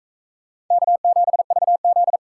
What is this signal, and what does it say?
Morse code, heard as a keyed tone, about 43 wpm, 700 Hz. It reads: K7VZ